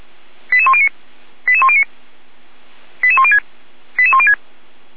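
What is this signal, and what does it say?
Five-tone selective call (ZVEI), heard as quick runs of five tones, tones of about 70 ms. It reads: 88109 88109 88106 88106
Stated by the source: ZVEI five-tone selective call received by SDR, recorded on the Feuerwehr Orlamünde audio server.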